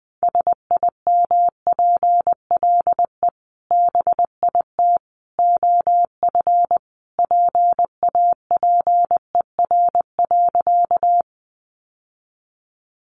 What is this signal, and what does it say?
Morse code, heard as a keyed tone, about 20 wpm, 700 Hz. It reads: SIMPLE BIT OF PAPER.
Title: Sentences from Top 900 Words 20wpm